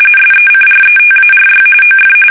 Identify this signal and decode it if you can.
RTTY, about 45.45 baud. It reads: SECRETARY OF
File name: fsk850 / usnavy5